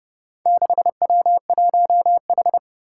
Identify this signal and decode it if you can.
Morse code, heard as a keyed tone, about 30 wpm, 700 Hz. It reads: 6W1H
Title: Four Challenging Letter-Number Combinations 30wpm